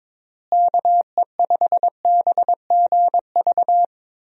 Morse code, heard as a keyed tone, about 22 wpm, 700 Hz. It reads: KE5BGV
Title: USA Call Signs - 6 Characters - Repeated 2x - 22wpm